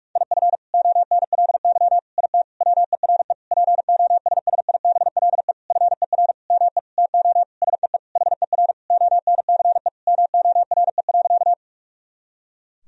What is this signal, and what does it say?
Morse code, heard as a keyed tone, about 43 wpm, 700 Hz. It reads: IF ONLY IT WERE POSSIBLE FER ME TO SEE HER ONCE MORE.
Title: Sentences from Top 500 Words and Abbreviations 45wpm